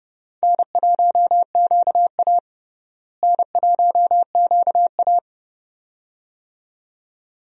Morse code, heard as a keyed tone, about 30 wpm, 700 Hz. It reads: N1QA N1QA
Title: USA Call Signs - 4 Characters - Repeated 2x - 30wpm